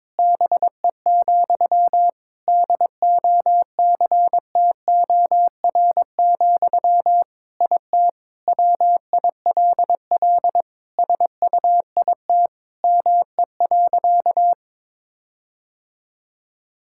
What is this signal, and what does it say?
Morse code, heard as a keyed tone, about 22 wpm, 700 Hz. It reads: BE, DOCTOR, IT WILL SUIT ME.
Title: Sentences from Top 1600 Words 22wpm